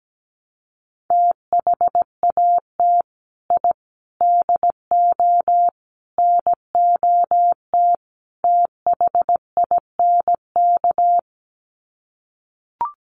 Morse code, heard as a keyed tone, about 17 wpm, 700 Hz. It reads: THAT I DO NOT THINK
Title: Sets of 5 Words Encore 17wpm